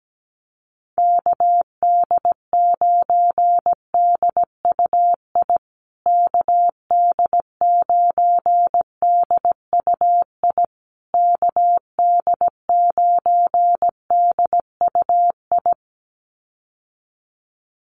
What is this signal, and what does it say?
Morse code, heard as a keyed tone, about 17 wpm, 700 Hz. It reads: KD9DUI KD9DUI KD9DUI
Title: USA Call Signs Repeated 3x 17wpm